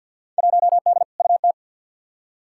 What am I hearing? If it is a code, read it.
Morse code, heard as a keyed tone, about 53 wpm, 700 Hz. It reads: 1D UT
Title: Field Day Exchanges - 50wpm